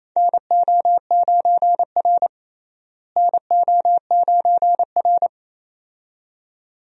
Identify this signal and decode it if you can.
Morse code, heard as a keyed tone, about 27 wpm, 700 Hz. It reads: NO9R NO9R
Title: USA Call Signs - 4 Characters - Repeated 2x - 28wpm